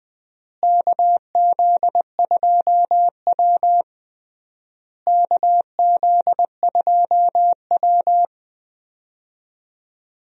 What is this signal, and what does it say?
Morse code, heard as a keyed tone, about 20 wpm, 700 Hz. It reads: KZ2W KZ2W
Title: USA Call Signs - 4 Characters - Repeated 3x - 20wpm